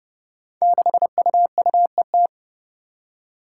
Morse code, heard as a keyed tone, about 32 wpm, 700 Hz. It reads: 6UUET
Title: Five Challenging Letter-Number Combinations 30wpm